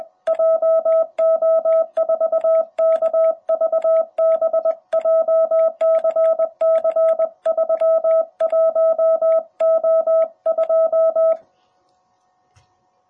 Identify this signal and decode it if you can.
Morse code, heard as a keyed tone, about 21 wpm, 660 Hz. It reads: JO4KVB JCC31O2
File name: jh8cbh / 2021cw